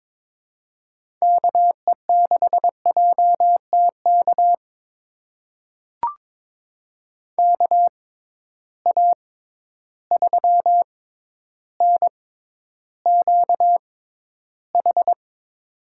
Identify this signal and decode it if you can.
Morse code, heard as a keyed tone, about 22 wpm, spaced as for 11 wpm, 700 Hz. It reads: KE6JTK KA3NQH
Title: USA Call Signs - 6 Characters - 6x Character Spacing - 22wpm